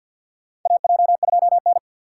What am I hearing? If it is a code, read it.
Morse code, heard as a keyed tone, about 53 wpm, 700 Hz. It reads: AJ2N
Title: USA Call Signs - 4 Characters - 2x Character Spacing - 50wpm